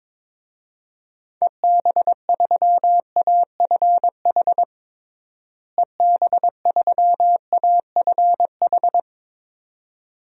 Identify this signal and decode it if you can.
Morse code, heard as a keyed tone, about 22 wpm, 700 Hz. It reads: EB3AFH EB3AFH
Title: DXCC-100 Call Signs - 6 Characters - Repeated 2x - 22wpm